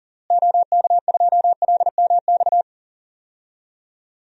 Morse code, heard as a keyed tone, about 39 wpm, 700 Hz. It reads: OK2LMX
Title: DXCC-100 Call Signs - 6 Characters - Repeated 3x - 40wpm